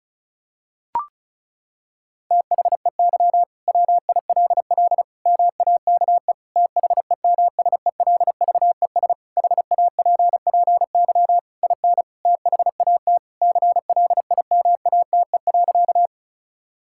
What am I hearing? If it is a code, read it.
Morse code, heard as a keyed tone, about 36 wpm, 700 Hz. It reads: THEY WILL MAKE THEMSELVES HAPPY IN THAT CLIMATE.
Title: Sentences from Top 1600 Words 35wpm